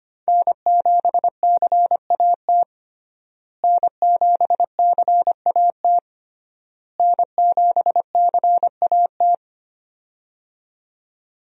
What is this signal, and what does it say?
Morse code, heard as a keyed tone, about 25 wpm, 700 Hz. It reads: N7CAT N7CAT N7CAT
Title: USA Call Signs - 5 Characters - Repeated 3x - 25wpm